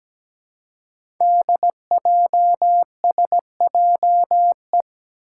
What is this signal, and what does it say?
Morse code, heard as a keyed tone, about 17 wpm, 700 Hz. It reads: DJSJE
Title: Five Challenging Letter-Number Combinations 17wpm